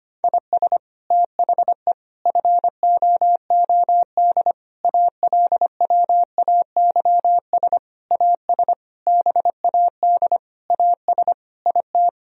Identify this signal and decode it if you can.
Morse code, heard as a keyed tone, about 25 wpm, 700 Hz. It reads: IS THE FOOD ALWAYS AS BAD AS IT
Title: Sentences from Top 1000 Words - Review Part 1 - 25wpm